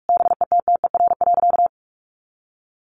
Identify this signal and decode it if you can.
Morse code, heard as a keyed tone, about 44 wpm, 700 Hz. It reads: BETTER.
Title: Easy Phrases 45wpm